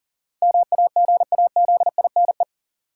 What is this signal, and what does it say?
Morse code, heard as a keyed tone, about 40 wpm, 700 Hz. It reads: MAGAZINE
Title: Sentences from Top 1000 Words - Review Part 7 - New Word Drill - 40wpm